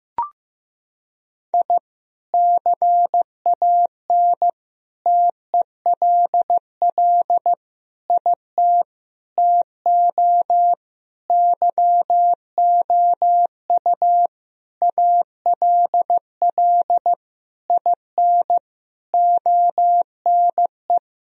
Morse code, heard as a keyed tone, about 15 wpm, 700 Hz. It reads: I CAN TELL IT TO YOU ALL IN ONE
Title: Sentences from Top 1000 Words - Review Part 1 - 15wpm